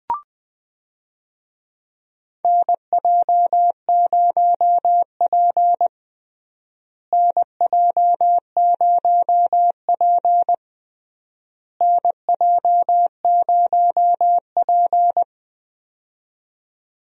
Morse code, heard as a keyed tone, about 20 wpm, 700 Hz. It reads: NJ0P NJ0P NJ0P
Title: USA Call Signs - 4 Characters - Repeated 3x - 20wpm